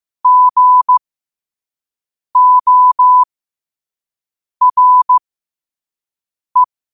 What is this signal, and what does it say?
Morse code, heard as a keyed tone, about 15 wpm, spaced as for 5 wpm, 1000 Hz. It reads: GORE
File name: qotdfiles / 5wpm